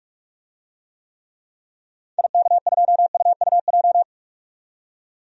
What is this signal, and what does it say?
Morse code, heard as a keyed tone, about 42 wpm, 700 Hz. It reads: IK2UUJ
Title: DXCC-100 Call Signs - 6 Characters - 45wpm